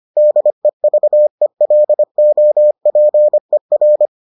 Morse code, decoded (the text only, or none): DEVELOPER